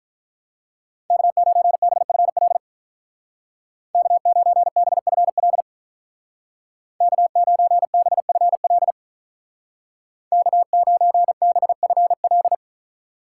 Morse code, heard as a keyed tone, about 45 wpm, 700 Hz. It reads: K9BFL K9BFL K9BFL K9BFL